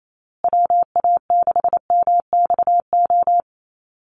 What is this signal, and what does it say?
Morse code, heard as a keyed tone, about 28 wpm, 700 Hz. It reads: WA6MXO
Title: USA Call Signs Repeated 2x 28wpm